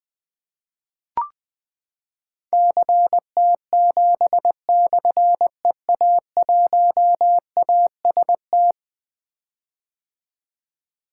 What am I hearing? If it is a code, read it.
Morse code, heard as a keyed tone, about 20 wpm, 700 Hz. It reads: CT7/EA1AST
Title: Difficult Call Signs 20wpm